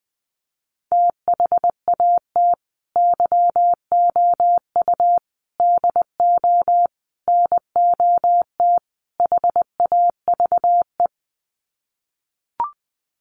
Morse code, heard as a keyed tone, about 20 wpm, 700 Hz. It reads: THAT YOU DO NOT HAVE
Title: Sets of 5 Words Encore 20wpm